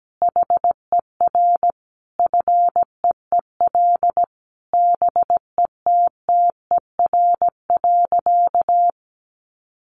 Morse code, heard as a keyed tone, about 17 wpm, 700 Hz. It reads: HER FEEL BETTER.
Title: Easy Phrases 17wpm